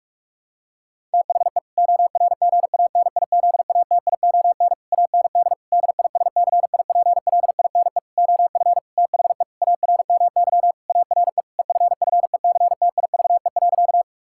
Morse code, heard as a keyed tone, about 46 wpm, 700 Hz. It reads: THE ORGANIZATION AND DISCIPLINE OF THE ARMY ARE EFFECTIVE.
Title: Sentences from Top 1000 Words - Review Part 6 - 45wpm